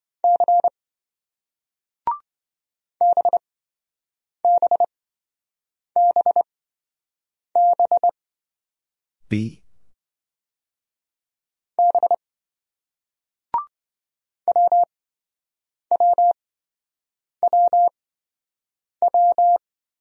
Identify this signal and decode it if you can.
Morse code, heard as a keyed tone, about 27 wpm, spaced as for 7 wpm, 700 Hz. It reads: C BBBB B WWWW